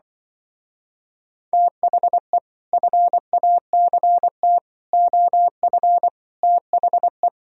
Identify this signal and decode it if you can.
Morse code, heard as a keyed tone, about 25 wpm, 700 Hz. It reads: THE FACT OF THE
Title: Speed-Racing - Sets of 5 Words 20wpm